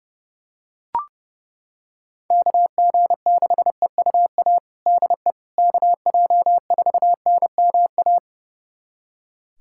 Morse code, heard as a keyed tone, about 32 wpm, 700 Hz. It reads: KG6EUA DE KJ4NMA
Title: USA Callsign DE Callsign - 6 Characters - 30wpm